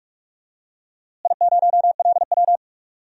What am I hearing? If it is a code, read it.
Morse code, heard as a keyed tone, about 47 wpm, 700 Hz. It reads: I0LW
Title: DXCC-100 Call Signs - 4 Characters - Repeated 3x - 45wpm